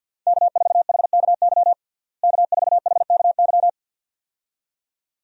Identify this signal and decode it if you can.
Morse code, heard as a keyed tone, about 51 wpm, 700 Hz. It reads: K4HKY K4HKY
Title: USA Call Signs - 5 Characters - Repeated 2x - 50wpm